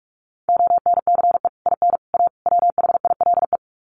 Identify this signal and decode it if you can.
Morse code, heard as a keyed tone, about 44 wpm, 700 Hz. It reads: ONCE IN A WHILE
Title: Sets of 4 Words 45wpm